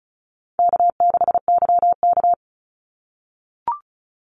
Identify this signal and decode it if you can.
Morse code, heard as a keyed tone, about 34 wpm, 700 Hz. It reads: K6YK
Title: NA SOTA Chaser Callsigns 35wpm